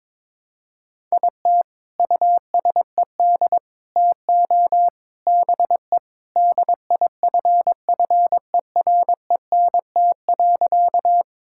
Morse code, heard as a keyed tone, about 22 wpm, 700 Hz. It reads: IT USED TO BE DIFFERENT.
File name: practice / 22wpm / Sentences with 4-Word Phrases - 22wpm